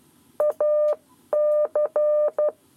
Morse code, heard as a keyed tone, about 12 wpm, 600 Hz. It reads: AC
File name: 7 Lezione cw 1